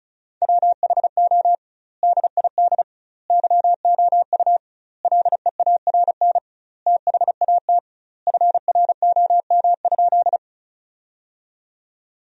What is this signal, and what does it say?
Morse code, heard as a keyed tone, about 34 wpm, 700 Hz. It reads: WHO DID YOU LEARN THAT FROM?
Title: 2x Word-Spacing - Sentences from Top 300 Words - 35wpm